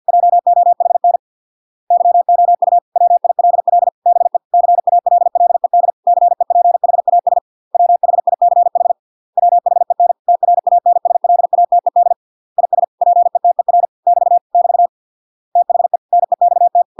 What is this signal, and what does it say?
Morse code, heard as a keyed tone, about 50 wpm, 700 Hz. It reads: JOHN YOU WILL BE CALLED CEPHAS WHICH WHEN TRANSLATED IS PETER <BT> <BT> THE NEXT